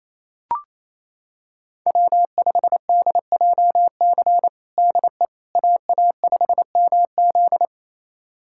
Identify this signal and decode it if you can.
Morse code, heard as a keyed tone, about 27 wpm, 700 Hz. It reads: W5DJC DE AA5MZ
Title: USA Callsign DE Callsign - 5 Characters - 28wpm